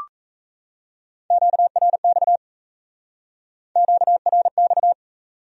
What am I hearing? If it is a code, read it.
Morse code, heard as a keyed tone, about 33 wpm, 700 Hz. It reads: QRX QRX
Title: Speed-Racing Q-Codes - 28wpm